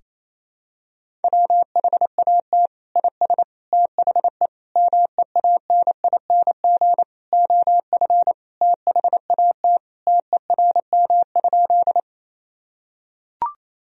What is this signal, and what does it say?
Morse code, heard as a keyed tone, about 28 wpm, 700 Hz. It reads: WHAT IS THE MEANING OF THAT TERM?
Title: Sentences from Top 1000 Words - Review Part 4 - 28wpm